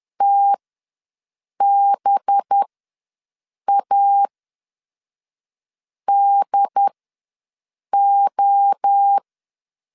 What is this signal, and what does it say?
Morse code, heard as a keyed tone, about 10 wpm, spaced as for 5 wpm, 800 Hz. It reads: TBADO
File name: LEZIONI / 9 2 ws 10wpm 4130